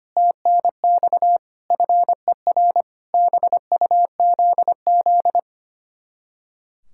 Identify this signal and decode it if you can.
Morse code, heard as a keyed tone, about 33 wpm, 700 Hz. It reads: TNX FER BUZZ